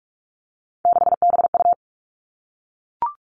Morse code, heard as a keyed tone, about 44 wpm, 700 Hz. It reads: 6BU
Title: All Cut Numbers - 3-Digit 45wpm